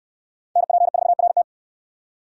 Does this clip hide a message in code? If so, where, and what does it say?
Yes: Morse code from 0.6 to 1.4 s, heard as a keyed tone, about 69 wpm, 700 Hz. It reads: NP4RT